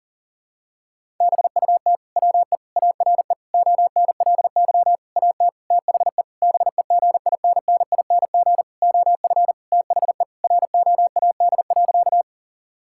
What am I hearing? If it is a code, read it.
Morse code, heard as a keyed tone, about 37 wpm, 700 Hz. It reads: BUT WE ARE ONLY AT THE BEGINNING OF THE ROAD.
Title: Sentences from Top 1000 Words - Review Part 5 - 40wpm